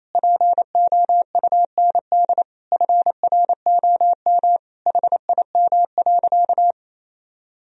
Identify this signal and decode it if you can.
Morse code, heard as a keyed tone, about 28 wpm, 700 Hz. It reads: POUND FROM HIM.
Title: Sentences from Top 1000 Words - Review Part 2 - 28wpm